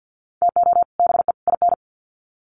Morse code, heard as a keyed tone, about 50 wpm, 700 Hz. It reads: TO BE IN
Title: Sets of 3 Words 50wpm